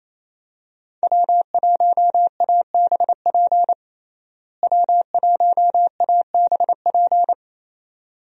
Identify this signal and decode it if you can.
Morse code, heard as a keyed tone, about 28 wpm, 700 Hz. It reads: W1ABP W1ABP
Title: USA Call Signs - 5 Characters - Repeated 2x - 28wpm